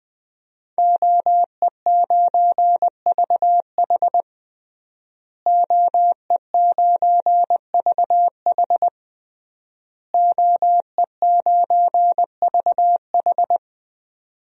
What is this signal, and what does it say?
Morse code, heard as a keyed tone, about 20 wpm, 700 Hz. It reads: OE9VH OE9VH OE9VH
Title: DXCC-100 Call Signs - 5 Characters - Repeated 3x - 20wpm